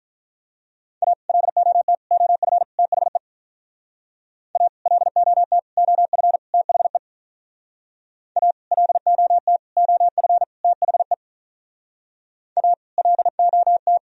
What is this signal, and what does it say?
Morse code, heard as a keyed tone, about 46 wpm, 700 Hz. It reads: A LOT OF THE A LOT OF THE A LOT OF THE A LOT